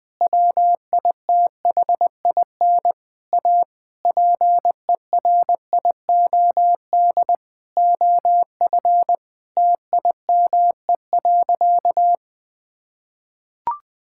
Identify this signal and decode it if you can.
Morse code, heard as a keyed tone, about 21 wpm, 700 Hz. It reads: WITHIN A PERIOD OF TIME.
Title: Sentences from Top 1000 Words - Review Part 5 - 20wpm